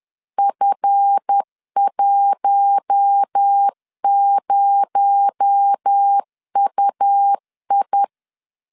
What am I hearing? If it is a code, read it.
Morse code, heard as a keyed tone, about 10 wpm, 800 Hz. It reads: F10UI
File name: LEZIONI / 8 YJ45 / 8 YJ45 10wpm 2549